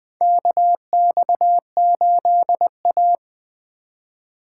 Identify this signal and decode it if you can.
Morse code, heard as a keyed tone, about 20 wpm, 700 Hz. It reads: KX8A
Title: USA Call Signs - 4 Characters - 6x Character Spacing - 20wpm